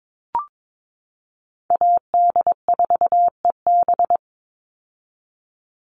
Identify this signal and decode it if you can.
Morse code, heard as a keyed tone, about 22 wpm, 700 Hz. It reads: AD4EB
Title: Call Signs - 22wpm